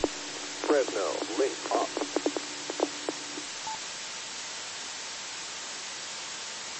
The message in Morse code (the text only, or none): none